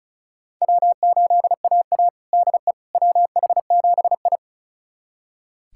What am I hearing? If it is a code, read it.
Morse code, heard as a keyed tone, about 36 wpm, 700 Hz. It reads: W8AA DE WH7I
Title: USA Callsign DE Callsign - 4 Characters - 35wpm